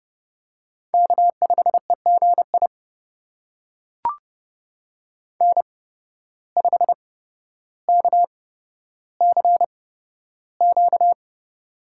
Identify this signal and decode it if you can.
Morse code, heard as a keyed tone, about 30 wpm, spaced as for 13 wpm, 700 Hz. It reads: K5EGI N5KCQ
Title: USA Call Signs - 5 Characters - 8x Character Spacing - 30wpm